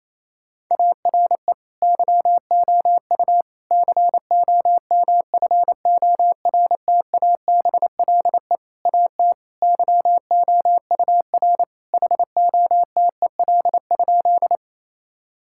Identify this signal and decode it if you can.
Morse code, heard as a keyed tone, about 28 wpm, 700 Hz. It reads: ARE YOU COMFORTABLE AT YOUR HOTEL?
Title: Sentences from Top 1600 Words 28wpm